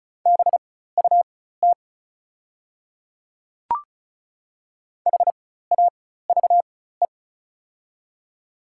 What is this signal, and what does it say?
Morse code, heard as a keyed tone, about 36 wpm, spaced as for 16 wpm, 700 Hz. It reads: BUT HAVE
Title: 4x Character Spacing - Top 100 Words - 35wpm